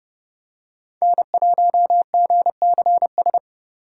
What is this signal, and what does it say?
Morse code, heard as a keyed tone, about 30 wpm, 700 Hz. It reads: N1GCS
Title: USA Call Signs - 5 Characters - 3x Character Spacing - 30wpm